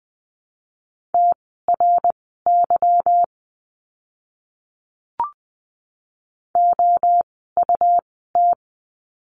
Morse code, heard as a keyed tone, about 20 wpm, spaced as for 15 wpm, 700 Hz. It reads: TRY OUT
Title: 160 Lesson 26f - Letter g - Words 20wpm